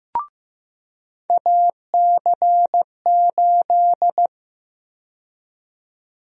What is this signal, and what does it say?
Morse code, heard as a keyed tone, about 15 wpm, 700 Hz. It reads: AC8
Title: USA Call Sign Prefixes - 15wpm